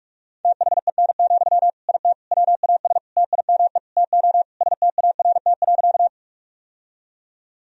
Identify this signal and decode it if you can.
Morse code, heard as a keyed tone, about 46 wpm, 700 Hz. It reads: THEN, IT WAS TIME TO START.
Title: Sentences from Top 1000 Words - Review Part 1 - 45wpm